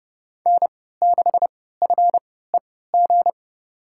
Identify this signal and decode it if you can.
Morse code, heard as a keyed tone, about 30 wpm, spaced as for 17 wpm, 700 Hz. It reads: N6FEG
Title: USA Call Signs - 5 Characters - 3x Character Spacing - 30wpm